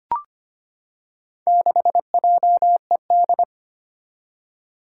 Morse code, heard as a keyed tone, about 25 wpm, 700 Hz. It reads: 6JED
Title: Four Challenging Letter-Number Combinations 25wpm